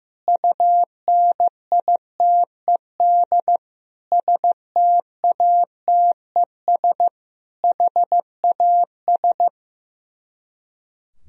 Morse code, heard as a keyed tone, about 15 wpm, 700 Hz. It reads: UNITED STATES HAS
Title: Speed-Racing - Sets of 4 Words 15wpm